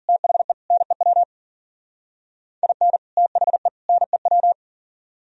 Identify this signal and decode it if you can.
Morse code, heard as a keyed tone, about 51 wpm, 700 Hz. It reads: THE NEW IN THE NEW